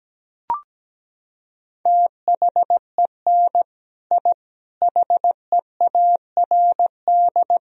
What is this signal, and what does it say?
Morse code, heard as a keyed tone, about 17 wpm, 700 Hz. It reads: THEN I HEARD
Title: Sentences from Top 1000 Words - Review Part 1 - 17wpm